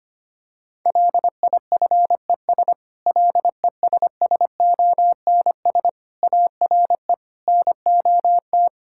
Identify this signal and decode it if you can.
Morse code, heard as a keyed tone, about 25 wpm, 700 Hz. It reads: LIFES LESSONS ARE NOT
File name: practice / 25wpm / Difficult Sentences with Hints - 25wpm